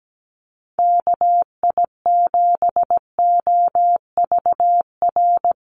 Morse code, heard as a keyed tone, about 17 wpm, 700 Hz. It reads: KI7OVR
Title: USA Call Signs Repeated 3x 17wpm